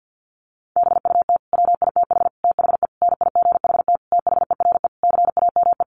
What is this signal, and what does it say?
Morse code, heard as a keyed tone, about 50 wpm, 700 Hz. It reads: BUT WITH THE NIGHT THERE CAME
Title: Sentences from Top 1500 Words 50wpm